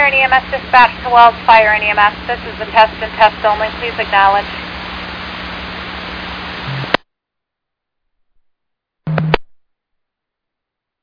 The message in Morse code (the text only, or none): none